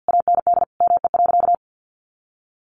Morse code, heard as a keyed tone, about 50 wpm, 700 Hz. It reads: AND ME.
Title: Sentences from Top 500 Words 50wpm